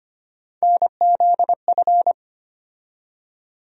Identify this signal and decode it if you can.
Morse code, heard as a keyed tone, about 25 wpm, 700 Hz. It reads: NZF
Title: USA Call Sign Postfixes - 3 Letters 25wpm